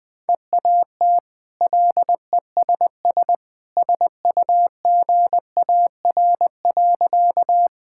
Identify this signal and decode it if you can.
Morse code, heard as a keyed tone, about 20 wpm, 700 Hz. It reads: EAT LESS SUGAR.
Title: Easy Sentences with Hints - 20wpm